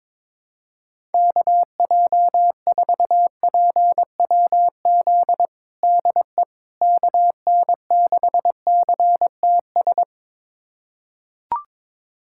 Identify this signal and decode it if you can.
Morse code, heard as a keyed tone, about 22 wpm, 700 Hz. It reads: KJ4PWZ DE KN6CTS
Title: USA Callsign DE Callsign 22wpm